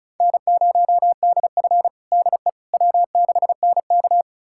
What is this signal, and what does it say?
Morse code, heard as a keyed tone, about 34 wpm, 700 Hz. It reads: N0DF DE W6NK
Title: USA Callsign DE Callsign - 4 Characters - 35wpm